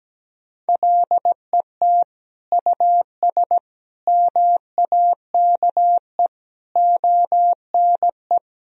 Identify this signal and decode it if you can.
Morse code, heard as a keyed tone, about 17 wpm, 700 Hz. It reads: LET US MAKE ONE
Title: Sentences from Top 1000 Words - Review Part 4 - 17wpm